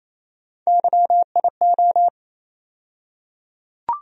Morse code, heard as a keyed tone, about 28 wpm, 700 Hz. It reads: YIO
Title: USA Call Sign Postfixes - 3 Letters 28wpm